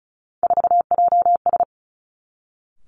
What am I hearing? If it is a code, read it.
Morse code, heard as a keyed tone, about 36 wpm, 700 Hz. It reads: 4JS
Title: Three Challenging Letter-Number Combinations 35wpm